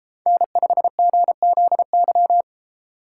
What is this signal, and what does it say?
Morse code, heard as a keyed tone, about 33 wpm, 700 Hz. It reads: N5GZY
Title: Speed Racing - USA 1x3 Call Signs 28wpm